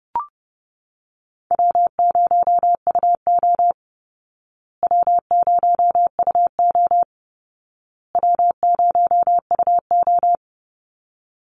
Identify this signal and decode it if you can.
Morse code, heard as a keyed tone, about 30 wpm, 700 Hz. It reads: W0UO W0UO W0UO
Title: Call Signs Repeated 2x - 30wpm